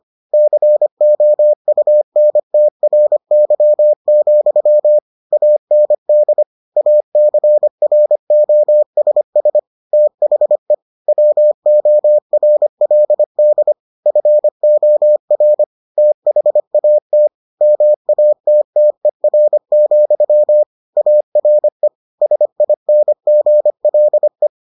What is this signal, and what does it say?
Morse code, heard as a keyed tone, about 25 wpm, 600 Hz. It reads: COUNTRY, AND ACROSS THE WORLD FOR THAT MATTER, ARE SINGLE